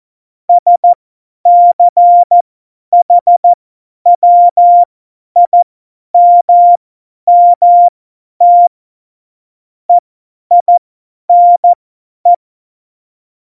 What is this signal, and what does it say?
Morse code, heard as a keyed tone, about 14 wpm, spaced as for 10 wpm, 700 Hz. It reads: SCHWIMMT EINE
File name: Morsekurs / Witze / Witz 25. Juni 2019, 50 F70